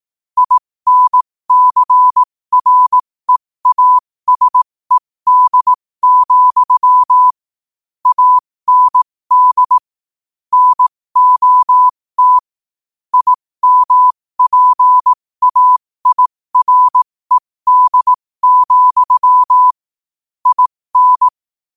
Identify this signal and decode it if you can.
Morse code, heard as a keyed tone, about 18 wpm, spaced as for 15 wpm, 1000 Hz. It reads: INCREASED, AND NOT IMPAIRED, IN
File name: qotdfiles / 15wpm